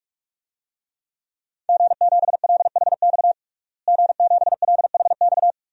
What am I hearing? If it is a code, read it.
Morse code, heard as a keyed tone, about 44 wpm, 700 Hz. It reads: G7LHX G7LHX
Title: DXCC-100 Call Signs - 5 Characters - Repeated 3x - 45wpm